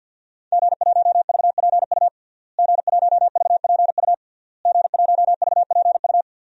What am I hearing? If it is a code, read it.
Morse code, heard as a keyed tone, about 53 wpm, 700 Hz. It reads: G1VPU G1VPU G1VPU
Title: DXCC-100 Call Signs - 5 Characters - Repeated 3x - 50wpm